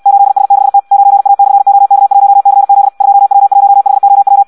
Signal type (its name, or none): Morse code